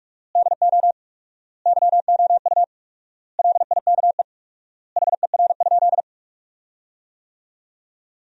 Morse code, heard as a keyed tone, about 48 wpm, 700 Hz. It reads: DO YOU LIKE HER?
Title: 4x Word-Spacing - Sentences from Top 100 Words - 45wpm